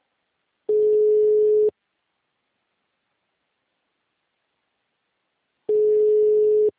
Ringback tone (Europe, 425 Hz).